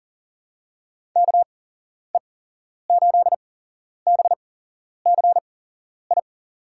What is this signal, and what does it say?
Morse code, heard as a keyed tone, about 40 wpm, spaced as for 11 wpm, 700 Hz. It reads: KE8BCI
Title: USA Call Signs - 6 Characters - 8x Character Spacing - 40wpm